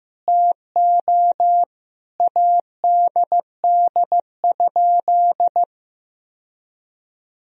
Morse code, heard as a keyed tone, about 15 wpm, 700 Hz. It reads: TO ADD?
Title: Sentences from Top 1000 Words - Review Part 1 - 15wpm